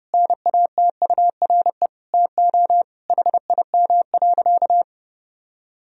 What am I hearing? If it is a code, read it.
Morse code, heard as a keyed tone, about 30 wpm, 700 Hz. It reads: NATURE TO HIM.
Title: Sentences from Top 1000 Words - Review Part 1 - 30wpm